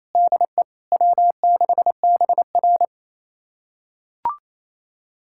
Morse code, heard as a keyed tone, about 28 wpm, 700 Hz. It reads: DE W6BR
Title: USA Callsign DE Callsign - 4 Characters - 28wpm